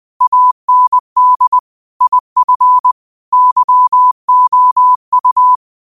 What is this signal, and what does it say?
Morse code, heard as a keyed tone, about 20 wpm, 1000 Hz. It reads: AND IF YOU
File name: qotdfiles / 20wpm